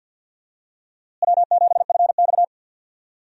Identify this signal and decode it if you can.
Morse code, heard as a keyed tone, about 51 wpm, 700 Hz. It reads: W7FX
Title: USA Call Signs - 4 Characters - Repeated 3x - 50wpm